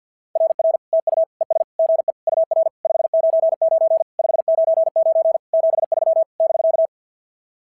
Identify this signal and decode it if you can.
Morse code, heard as a keyed tone, about 46 wpm, 660 Hz. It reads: RR TU ES GE UR 599 599 73 <BK>
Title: SOTA QSO Practice 50wpm